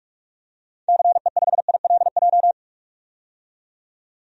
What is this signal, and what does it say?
Morse code, heard as a keyed tone, about 46 wpm, 700 Hz. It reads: CE5ILJ
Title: DXCC-100 Call Signs - 6 Characters - 45wpm